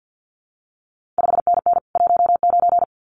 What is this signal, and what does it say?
Morse code, heard as a keyed tone, about 50 wpm, 700 Hz. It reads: 5NN 19
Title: CQ WW DX Contest Exchange - 50wpm